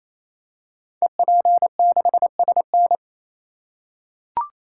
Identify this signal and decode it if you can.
Morse code, heard as a keyed tone, about 27 wpm, 700 Hz. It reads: EP6SN
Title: Code Groups - 5 Letter:Numbers - 28wpm